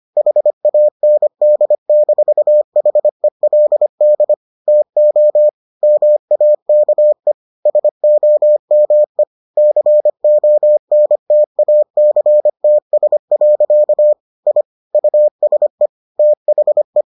Morse code, heard as a keyed tone, about 25 wpm, 600 Hz. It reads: HAND-HELD TO MAKE SOME CONTACTS. I USE THE